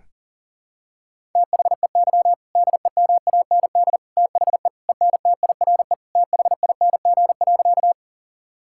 Morse code, heard as a keyed tone, about 40 wpm, 700 Hz. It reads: THEY DEMAND THE ENTIRE THING.
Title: Sentences from Top 1000 Words - Review Part 5 - 40wpm